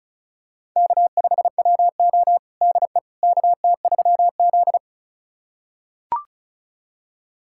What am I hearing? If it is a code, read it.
Morse code, heard as a keyed tone, about 36 wpm, 700 Hz. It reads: K5WO DE KT3Z
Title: USA Callsign DE Callsign - 4 Characters - 35wpm